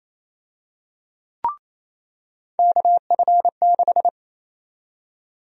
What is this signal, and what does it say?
Morse code, heard as a keyed tone, about 28 wpm, 700 Hz. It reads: KF6
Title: USA Call Sign Prefixes - 28wpm